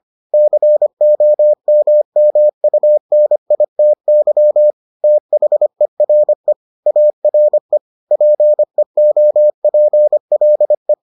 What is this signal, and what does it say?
Morse code, heard as a keyed tone, about 25 wpm, 600 Hz. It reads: COMMUNITY THERE ARE PEOPLE